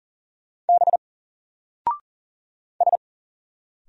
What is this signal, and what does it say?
Morse code, heard as a keyed tone, about 39 wpm, 700 Hz. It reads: B S
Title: Single Letter-Number 40wpm